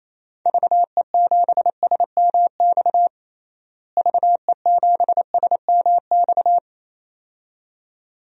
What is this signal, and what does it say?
Morse code, heard as a keyed tone, about 28 wpm, 700 Hz. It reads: VE7SMX VE7SMX
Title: DXCC-100 Call Signs - 6 Characters - Repeated 2x - 28wpm